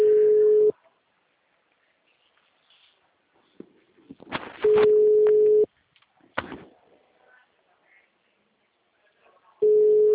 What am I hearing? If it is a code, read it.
Ringback tone (Europe, 425 Hz).
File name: out- -2005- - - .49257